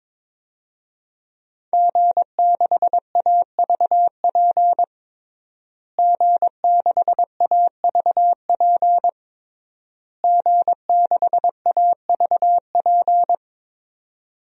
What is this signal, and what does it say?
Morse code, heard as a keyed tone, about 22 wpm, 700 Hz. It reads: G6AVP G6AVP G6AVP
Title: DXCC-100 Call Signs - 5 Characters - Repeated 3x - 22wpm